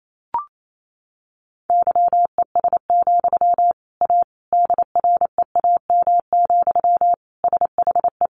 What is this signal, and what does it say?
Morse code, heard as a keyed tone, about 28 wpm, 700 Hz. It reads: YES, A DREAM, SHE
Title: Sentences from Top 1500 Words 28wpm